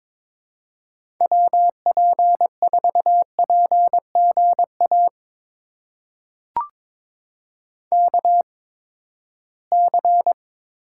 Morse code, heard as a keyed tone, about 22 wpm, spaced as for 15 wpm, 700 Hz. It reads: WP4PGA KC
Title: USA Call Signs - 8x Character Spacing - 22wpm